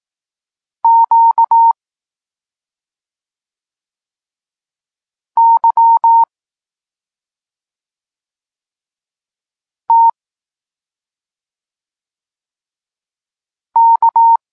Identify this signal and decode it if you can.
Morse code, heard as a keyed tone, about 18 wpm, 920 Hz. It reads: Q Y T K
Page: Tekst 32 – Doskonalenie odbioru poznanych liter